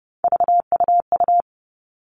Morse code, heard as a keyed tone, about 30 wpm, 700 Hz. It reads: VUU
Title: All Cut Numbers - 3-Digit 30wpm